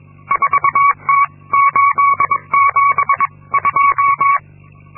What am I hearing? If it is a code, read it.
Morse code, heard as a keyed tone, about 22 wpm, spaced as for 19 wpm, 1120 Hz. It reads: 4T872
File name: Linksmakalnis-2